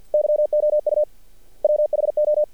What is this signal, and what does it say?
Morse code, heard as a keyed tone, about 48 wpm, 600 Hz. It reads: YOU WHO
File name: archived / CRSnet